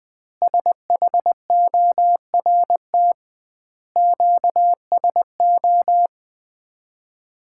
Morse code, heard as a keyed tone, about 20 wpm, 700 Hz. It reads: SHORT QSO
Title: Common QSO Phrases - 2x Word-Spacing - 20wpm